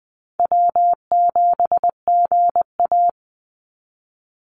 Morse code, heard as a keyed tone, about 20 wpm, 700 Hz. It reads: W7GA
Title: NA SOTA Chaser Callsigns 20wpm